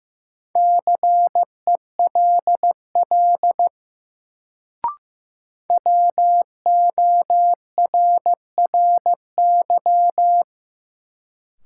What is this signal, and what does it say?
Morse code, heard as a keyed tone, about 15 wpm, 700 Hz. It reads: CELL WORRY
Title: Sentences from Top 1100 Words - New Word Drill Alt - 15wpm